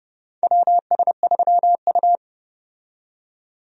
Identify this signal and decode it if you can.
Morse code, heard as a keyed tone, about 30 wpm, 700 Hz. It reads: WS3U
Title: USA Call Signs - 4 Characters - 30wpm